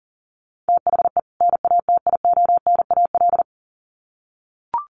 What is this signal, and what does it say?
Morse code, heard as a keyed tone, about 40 wpm, 700 Hz. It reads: THE NATIONAL
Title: Sets of 2 Words 40wpm